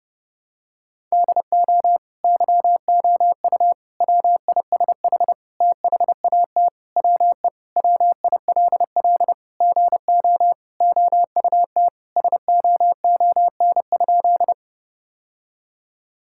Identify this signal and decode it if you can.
Morse code, heard as a keyed tone, about 30 wpm, 700 Hz. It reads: DO YOU WISH THAT WE WILL GO OUT SOON?
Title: Sentences from Top 1600 Words 30wpm